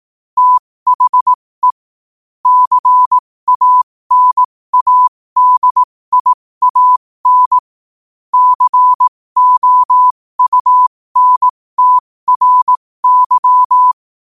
Morse code, heard as a keyed tone, about 18 wpm, spaced as for 15 wpm, 1000 Hz. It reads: THE CANADIAN COUNTRY
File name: qotdfiles / 15wpm